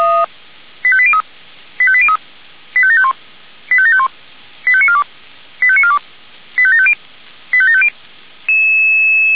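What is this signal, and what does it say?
Five-tone selective call (ZVEI), heard as quick runs of five tones, tones of about 70 ms. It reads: 85803 85803 85751 85751 85942 85942 85750 85750
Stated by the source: ZVEI five-tone selective call received on an SDR.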